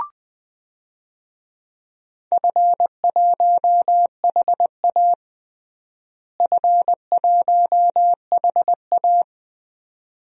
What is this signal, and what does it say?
Morse code, heard as a keyed tone, about 20 wpm, 700 Hz. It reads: F1HA F1HA
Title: DXCC-100 Call Signs - 4 Characters - Repeated 2x - 20wpm